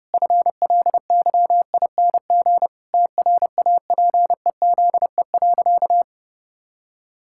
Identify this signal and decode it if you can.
Morse code, heard as a keyed tone, about 30 wpm, 700 Hz. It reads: FLYING TRAPEZE.
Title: Pangram Challenge v2 - 30wpm